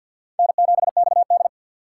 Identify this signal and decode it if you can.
Morse code, heard as a keyed tone, about 53 wpm, 700 Hz. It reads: N7XD
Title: USA Call Signs - 4 Characters - 6x Character Spacing - 50wpm